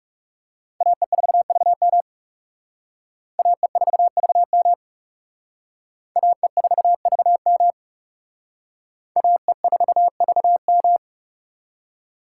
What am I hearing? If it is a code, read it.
Morse code, heard as a keyed tone, about 36 wpm, 700 Hz. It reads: AE4VM AE4VM AE4VM AE4VM
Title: Speed Racing - USA 2x2 Call Signs 30wpm